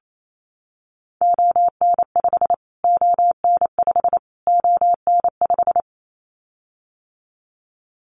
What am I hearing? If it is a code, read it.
Morse code, heard as a keyed tone, about 28 wpm, 700 Hz. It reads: ON5 ON5 ON5
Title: Callsign Prefixes for Top 25 DXCC - Repeated 3x - 28wpm